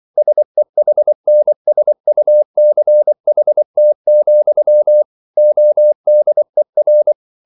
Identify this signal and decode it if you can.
Morse code, heard as a keyed tone, about 24 wpm, 600 Hz. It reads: SEHNSUCHT, ODER